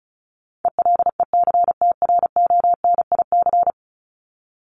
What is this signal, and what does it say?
Morse code, heard as a keyed tone, about 34 wpm, 700 Hz. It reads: ELECTRONIC
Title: Top 1001-1250 Words 35wpm